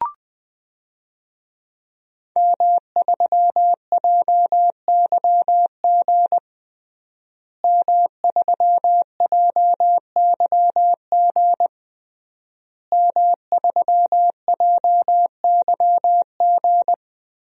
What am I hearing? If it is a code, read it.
Morse code, heard as a keyed tone, about 20 wpm, 700 Hz. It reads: M3JYG M3JYG M3JYG
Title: DXCC-100 Call Signs - 5 Characters - Repeated 3x - 20wpm